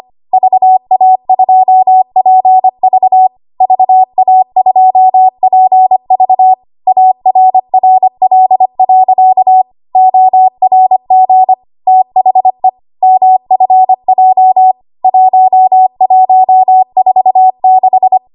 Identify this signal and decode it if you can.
Morse code, heard as a keyed tone, about 25 wpm, 750 Hz. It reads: VA2PV VA2PV ARRL.ORG THE MFJ 1146
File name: Morse / 25WPM